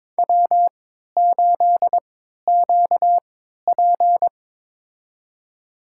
Morse code, heard as a keyed tone, about 22 wpm, spaced as for 13 wpm, 700 Hz. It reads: W8QP
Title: USA Call Signs - 4 Characters - 3x Character Spacing - 22wpm